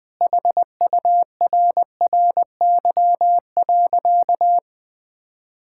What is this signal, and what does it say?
Morse code, heard as a keyed tone, about 20 wpm, 700 Hz. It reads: HURRY.
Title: Sentences with 4-Word Phrases - Spoken Beginning - 20wpm